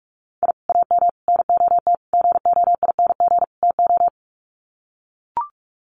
Morse code, heard as a keyed tone, about 44 wpm, 700 Hz. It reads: I AM NOT GOING TO
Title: Sets of 5 Words Encore 45wpm